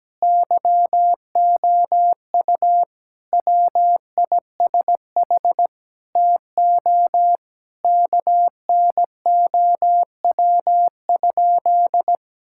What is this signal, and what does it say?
Morse code, heard as a keyed tone, about 17 wpm, 700 Hz. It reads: YOU WISH TO KNOW?
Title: Sentences from Top 1600 Words 17wpm